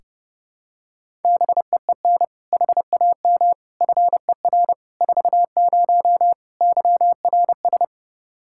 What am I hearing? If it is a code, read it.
Morse code, heard as a keyed tone, about 30 wpm, 700 Hz. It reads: BEEN HAM FER 40 YRS
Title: Practice QSO - Exchange 3 - More Info - 30wpm